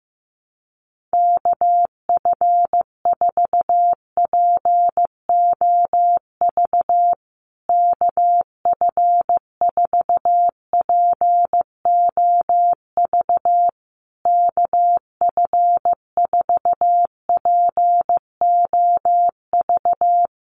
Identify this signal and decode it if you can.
Morse code, heard as a keyed tone, about 15 wpm, 700 Hz. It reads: KF4POV KF4POV KF4POV
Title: USA Call Signs Repeated 3x 15wpm